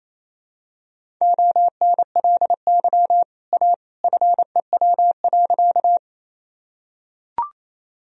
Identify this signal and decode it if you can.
Morse code, heard as a keyed tone, about 28 wpm, 700 Hz. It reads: ONLY A FEW.
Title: Sentences from Top 1000 Words - Review Part 1 - 28wpm